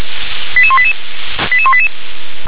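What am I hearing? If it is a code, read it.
Five-tone selective call (ZVEI), heard as quick runs of five tones, tones of about 70 ms. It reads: 88188 88188
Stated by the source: ZVEI five-tone call 88080 received on an SDR.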